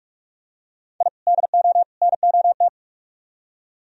Morse code, heard as a keyed tone, about 43 wpm, 700 Hz. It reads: I DO NOT